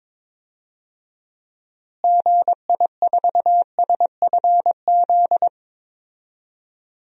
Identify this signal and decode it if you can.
Morse code, heard as a keyed tone, about 22 wpm, 700 Hz. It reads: GI4SFZ
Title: DXCC-100 Call Signs - 6 Characters - Repeated 2x - 22wpm